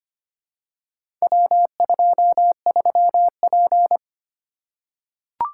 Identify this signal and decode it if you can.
Morse code, heard as a keyed tone, about 25 wpm, 700 Hz. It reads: W23P
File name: practice / 25wpm / Four Letter-Number Combinations - 25wpm